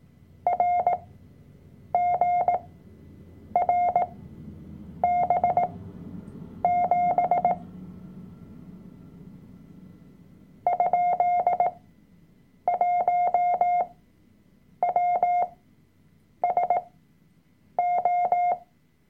Morse code, heard as a keyed tone, about 18 wpm, spaced as for 7 wpm, 710 Hz. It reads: RGRB7 ?1WSO
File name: Workshop Dateien / CW-Online-Kurs / Lektion 003